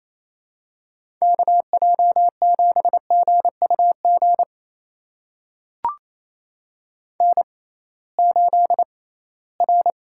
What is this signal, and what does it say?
Morse code, heard as a keyed tone, about 28 wpm, spaced as for 18 wpm, 700 Hz. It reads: KJ7GUG N8R
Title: USA Call Signs - 6x Character Spacing - 28wpm